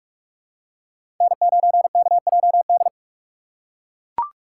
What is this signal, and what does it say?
Morse code, heard as a keyed tone, about 41 wpm, 700 Hz. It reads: N9KJD
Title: USA Call Signs - 6x Character Spacing - 45wpm